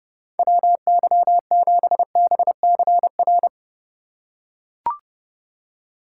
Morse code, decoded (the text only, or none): WY7BCR